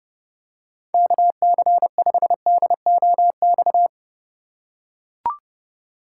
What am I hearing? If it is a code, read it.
Morse code, heard as a keyed tone, about 30 wpm, 700 Hz. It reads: KC5DOX
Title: USA Call Signs - 6 Characters - 3x Character Spacing - 30wpm